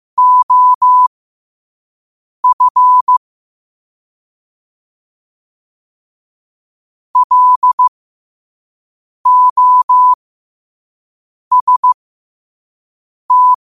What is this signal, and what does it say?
Morse code, heard as a keyed tone, about 15 wpm, spaced as for 5 wpm, 1000 Hz. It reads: OF LOST
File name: qotdfiles / 5wpm